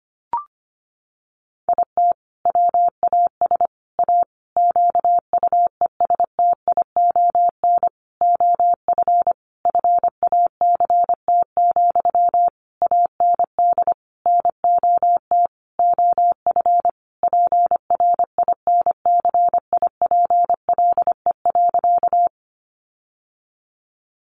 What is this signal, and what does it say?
Morse code, heard as a keyed tone, about 25 wpm, 700 Hz. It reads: IT WAS A QUESTION OF FACT, AND NOT OF PRINCIPLE.
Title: Sentences from Top 1400 Words 25wpm